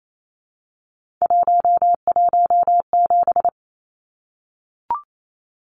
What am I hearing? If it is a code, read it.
Morse code, heard as a keyed tone, about 28 wpm, 700 Hz. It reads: 117
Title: Numbers - 3-Digit 28wpm